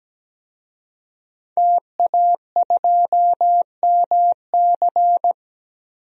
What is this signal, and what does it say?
Morse code, heard as a keyed tone, about 17 wpm, 700 Hz. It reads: TA2MC
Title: DXCC-100 Call Signs - 5 Characters - Repeated 3x - 17wpm